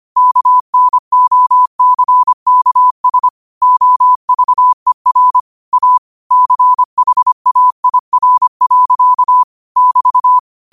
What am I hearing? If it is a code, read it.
Morse code, heard as a keyed tone, about 25 wpm, 1000 Hz. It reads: KNOCKS OVER A CHAIR. <BT>
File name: qotdfiles / 25wpm